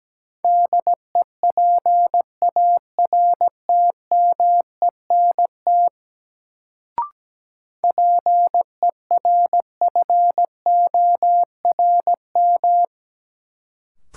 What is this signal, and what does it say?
Morse code, heard as a keyed tone, about 17 wpm, 700 Hz. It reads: DEPARTMENT PERFORM
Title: Sentences from Top 1000 Words - Review Part 5 - New Word Drill - 17wpm